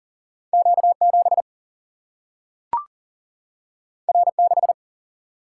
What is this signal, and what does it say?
Morse code, heard as a keyed tone, about 39 wpm, 700 Hz. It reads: Q7 R6
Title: Single-Letter Single-Number 40wpm